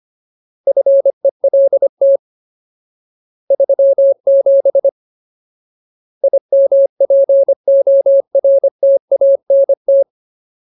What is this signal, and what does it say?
Morse code, heard as a keyed tone, about 25 wpm, 550 Hz. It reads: FELT 37 IMPORTANT
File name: Ultimate Word List / 25WPM Ultimate